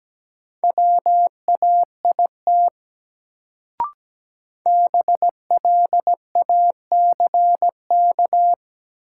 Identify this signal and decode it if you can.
Morse code, heard as a keyed tone, about 17 wpm, 700 Hz. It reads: WAIT BLACK
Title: Sets of 5 Words Encore - New Word Drill - 17wpm